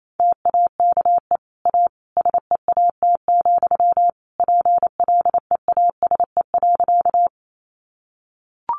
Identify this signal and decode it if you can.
Morse code, heard as a keyed tone, about 28 wpm, 700 Hz. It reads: TAKE A SEAT, PLEASE.
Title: Sentences from Top 1500 Words 28wpm